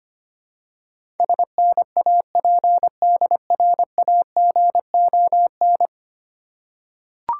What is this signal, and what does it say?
Morse code, heard as a keyed tone, about 25 wpm, 700 Hz. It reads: SNAPDRAGON
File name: practice / 25wpm / Difficult Words with Hints - 25wpm